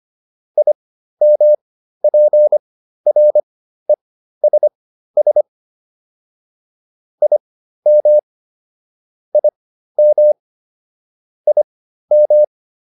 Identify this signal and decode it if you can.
Morse code, heard as a keyed tone, about 25 wpm, spaced as for 13 wpm, 600 Hz. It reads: IMPRESS IM IM IM
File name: IM7 13